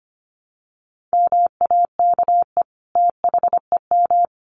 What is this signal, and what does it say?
Morse code, heard as a keyed tone, about 25 wpm, 700 Hz. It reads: MAKE THEM
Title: Sentences from Top 900 Words 25wpm